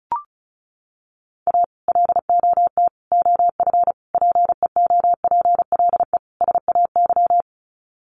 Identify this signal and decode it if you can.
Morse code, heard as a keyed tone, about 34 wpm, 700 Hz. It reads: A LOT OF PEOPLE SAY
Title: Sets of 5 Words Encore2 35wpm